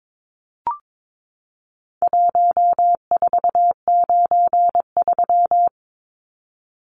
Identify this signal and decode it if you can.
Morse code, heard as a keyed tone, about 22 wpm, 700 Hz. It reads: 1493
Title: Numbers - Contest Style - 4-Digit 22wpm